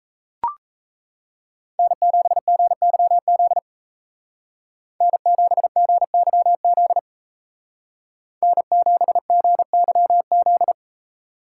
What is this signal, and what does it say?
Morse code, heard as a keyed tone, about 36 wpm, 700 Hz. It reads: N7GYZ N7GYZ N7GYZ